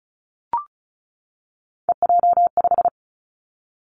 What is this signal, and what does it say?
Morse code, heard as a keyed tone, about 36 wpm, 700 Hz. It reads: EJ5
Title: Three Challenging Letter-Number Combinations 35wpm